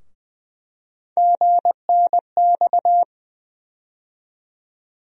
Morse code, heard as a keyed tone, about 20 wpm, 700 Hz. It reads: GNX
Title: USA Call Sign Postfixes - 3 Letters 20wpm